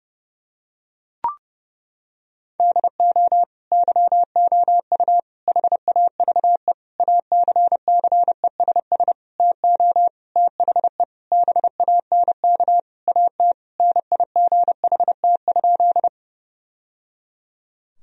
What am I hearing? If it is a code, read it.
Morse code, heard as a keyed tone, about 32 wpm, 700 Hz. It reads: DO YOU HAVE ACCESS TO THE BANK AT NIGHT?
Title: Sentences from Top 1000 Words - Review Part 4 - 30wpm